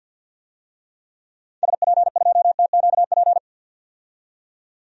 Morse code, heard as a keyed tone, about 51 wpm, 700 Hz. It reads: SP2TQP
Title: DXCC-100 Call Signs - 6 Characters - 50wpm